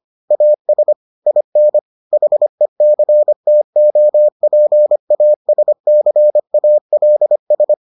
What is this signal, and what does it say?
Morse code, heard as a keyed tone, about 25 wpm, 600 Hz. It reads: AS IN HECTOPASCALS